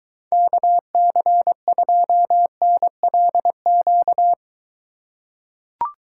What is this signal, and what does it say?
Morse code, heard as a keyed tone, about 23 wpm, 700 Hz. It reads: KC2NLQ